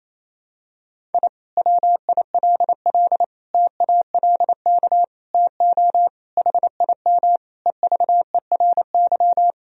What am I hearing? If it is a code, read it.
Morse code, heard as a keyed tone, about 28 wpm, 700 Hz. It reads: I WILL TALK TO HIM EVERY
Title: Sentences from Top 1000 Words - Review Part 2 - 28wpm